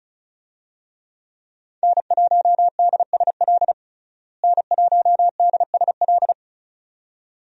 Morse code, heard as a keyed tone, about 36 wpm, 700 Hz. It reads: N1DSL N1DSL
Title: USA Call Signs - 5 Characters - Repeated 2x - 35wpm